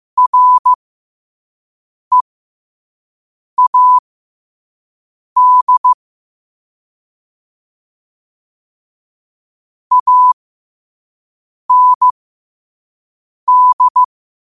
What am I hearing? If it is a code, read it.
Morse code, heard as a keyed tone, about 15 wpm, spaced as for 5 wpm, 1000 Hz. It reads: READ AND
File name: qotdfiles / 5wpm